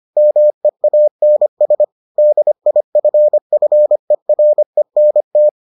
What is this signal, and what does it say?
Morse code, heard as a keyed tone, about 25 wpm, 600 Hz. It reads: MEANS DIFFERENT